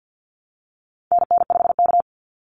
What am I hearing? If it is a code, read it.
Morse code, heard as a keyed tone, about 50 wpm, 700 Hz. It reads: NN5K
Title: NA SOTA Activator Callsigns 50wpm